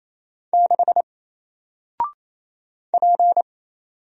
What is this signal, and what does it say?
Morse code, heard as a keyed tone, about 27 wpm, 700 Hz. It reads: 6 P
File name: Single-Letter-Number-28wpm